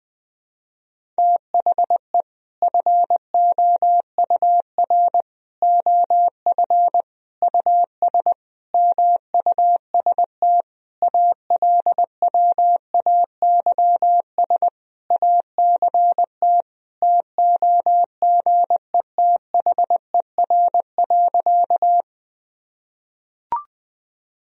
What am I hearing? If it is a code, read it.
Morse code, heard as a keyed tone, about 20 wpm, 700 Hz. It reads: THE FOUR OF US MUST ALWAYS ACT TOGETHER.
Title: Sentences from Top 1000 Words - Review Part 1 - 20wpm